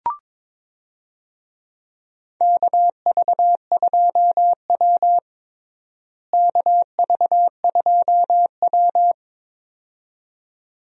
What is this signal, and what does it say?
Morse code, heard as a keyed tone, about 22 wpm, 700 Hz. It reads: KV2W KV2W
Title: USA Call Signs - 4 Characters - Repeated 2x - 22wpm